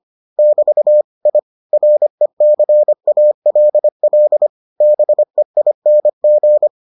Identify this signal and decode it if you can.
Morse code, heard as a keyed tone, about 25 wpm, 600 Hz. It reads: <BT> I RECALL BEING